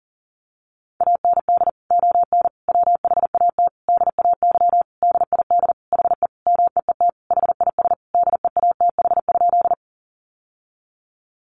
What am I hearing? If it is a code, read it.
Morse code, heard as a keyed tone, about 40 wpm, 700 Hz. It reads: AND ON WHAT DAY DID HE MEET HIS DEATH?
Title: Sentences from Top 1100 Words 40wpm